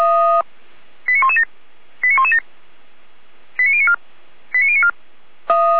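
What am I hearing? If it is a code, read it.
Five-tone selective call (ZVEI), heard as quick runs of five tones, tones of about 70 ms. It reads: 89117 89117 89094 89094
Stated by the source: ZVEI five-tone selective call received on SDR.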